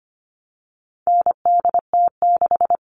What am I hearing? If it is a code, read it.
Morse code, heard as a keyed tone, about 25 wpm, 700 Hz. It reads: NDT6
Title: All Cut Numbers - 4-Digit 25wpm